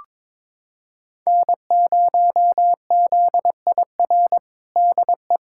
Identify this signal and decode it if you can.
Morse code, heard as a keyed tone, about 22 wpm, 700 Hz. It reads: N0ZIR DE
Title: USA Callsign DE Callsign - 5 Characters - 22wpm